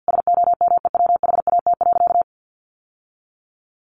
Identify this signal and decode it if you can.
Morse code, heard as a keyed tone, about 50 wpm, 700 Hz. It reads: SOMEWHAT.
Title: Sentences from Top 1300 Words 50wpm